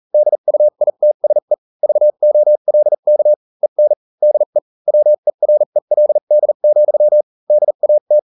Morse code, heard as a keyed tone, about 40 wpm, spaced as for 35 wpm, 600 Hz. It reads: DUITSE VOLK EN DE WERELD, DAT